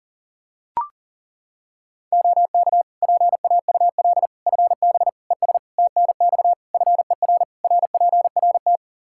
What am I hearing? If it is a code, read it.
Morse code, heard as a keyed tone, about 37 wpm, 700 Hz. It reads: OK PAUL FB ES TNX FER RPRT
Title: K1USN SST QSO Practice 40wpm